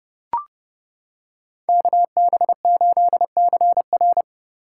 Morse code, heard as a keyed tone, about 29 wpm, 700 Hz. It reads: KB8CR